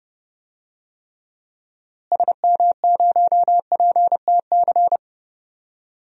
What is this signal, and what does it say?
Morse code, heard as a keyed tone, about 32 wpm, 700 Hz. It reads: SM0PTC
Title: DXCC-100 Call Signs - 6 Characters - Repeated 3x - 30wpm